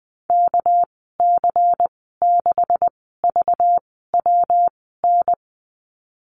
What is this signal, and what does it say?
Morse code, heard as a keyed tone, about 20 wpm, spaced as for 15 wpm, 700 Hz. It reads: KC6VWN
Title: 213 Lesson 34g - Number 8 - Call Signs 20wpm